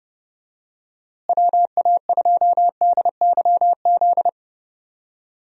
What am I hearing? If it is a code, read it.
Morse code, heard as a keyed tone, about 32 wpm, 700 Hz. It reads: WA2DYZ
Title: USA Call Signs - 6 Characters - Repeated 3x - 30wpm